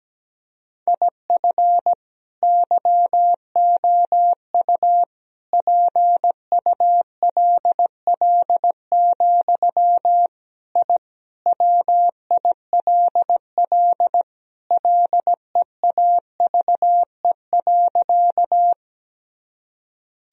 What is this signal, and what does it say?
Morse code, heard as a keyed tone, about 17 wpm, 700 Hz. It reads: IF YOU PULL, I WILL LEAVE.
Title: Sentences from Top 1000 Words - Review Part 2 - 17wpm